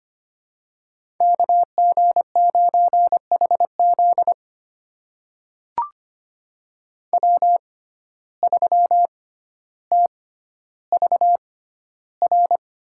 Morse code, heard as a keyed tone, about 25 wpm, spaced as for 13 wpm, 700 Hz. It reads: KG9HZ W3TVR
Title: USA Call Signs - 5 Characters - 6x Character Spacing - 25wpm